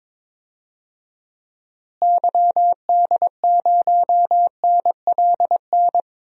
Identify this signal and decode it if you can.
Morse code, heard as a keyed tone, about 22 wpm, 700 Hz. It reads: YD0NLN
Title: DXCC-100 Call Signs - 6 Characters - Repeated 3x - 22wpm